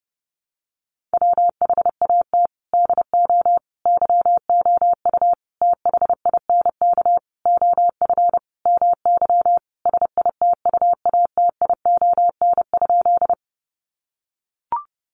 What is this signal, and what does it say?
Morse code, heard as a keyed tone, about 30 wpm, 700 Hz. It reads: WHAT DO YOU THINK OF MY SITUATION?
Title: Sentences from Top 1200 Words 30wpm